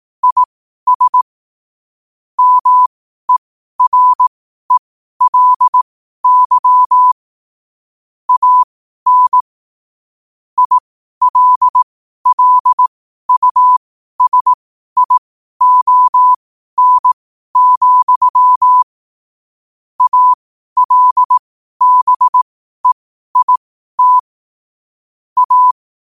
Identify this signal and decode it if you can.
Morse code, heard as a keyed tone, about 18 wpm, spaced as for 12 wpm, 1000 Hz. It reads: IS MERELY AN ILLUSION, ALBEIT A
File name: qotdfiles / 12wpm